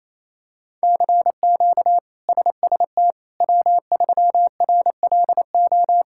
Morse code, heard as a keyed tone, about 28 wpm, 700 Hz. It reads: CQ SST W3RLO
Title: K1USN SST QSO Practice - 28wpm